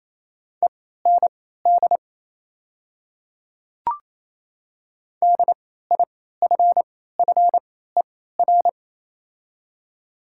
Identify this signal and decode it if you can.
Morse code, heard as a keyed tone, about 28 wpm, spaced as for 16 wpm, 700 Hz. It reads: END DIFFER
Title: 3x Character Spacing - Top 300 Words - 28wpm